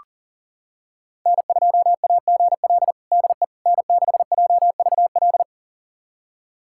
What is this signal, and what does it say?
Morse code, heard as a keyed tone, about 37 wpm, 700 Hz. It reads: N2AGL DE N6JVL
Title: USA Callsign DE Callsign - 5 Characters - 40wpm